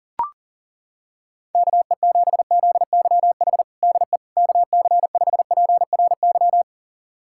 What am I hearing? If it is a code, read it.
Morse code, heard as a keyed tone, about 39 wpm, 700 Hz. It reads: KE7ZYH DE KC5PRY
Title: USA Callsign DE Callsign - 6 Characters - 40wpm